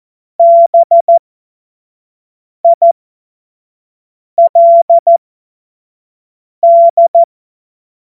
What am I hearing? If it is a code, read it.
Morse code, heard as a keyed tone, about 14 wpm, spaced as for 5 wpm, 680 Hz. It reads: BILD
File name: Morsekurs / Witze / Witz 20. Januar 2026 25 F70